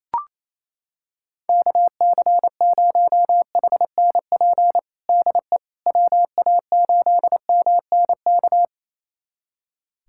Morse code, heard as a keyed tone, about 28 wpm, 700 Hz. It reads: KC0HNP DE WA8MNK
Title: USA Callsign DE Callsign 28wpm